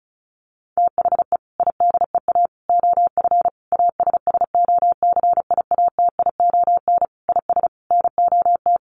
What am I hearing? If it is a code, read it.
Morse code, heard as a keyed tone, about 34 wpm, 700 Hz. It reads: THE IDEA OF ASSOCIATION IS NOT
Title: Sentences from Top 1100 Words 35wpm